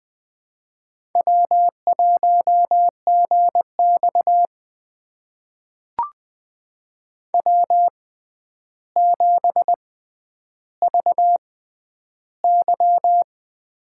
Morse code, heard as a keyed tone, about 20 wpm, spaced as for 10 wpm, 700 Hz. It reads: W1GX W7VY
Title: USA Call Signs - 4 Characters - 6x Character Spacing - 20wpm